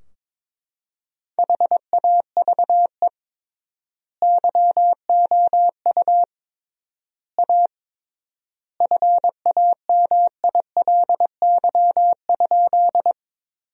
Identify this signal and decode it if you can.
Morse code, heard as a keyed tone, about 22 wpm, 700 Hz. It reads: HAVE YOU A FAMILY?
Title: 3x Word-Spacing - Sentences from Top 400 Words - 22wpm